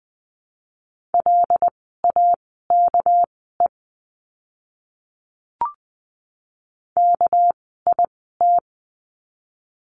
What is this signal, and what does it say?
Morse code, heard as a keyed tone, about 20 wpm, spaced as for 14 wpm, 700 Hz. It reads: LAKE KIT
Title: 199 Lesson 32f - Letter k - Words 20wpm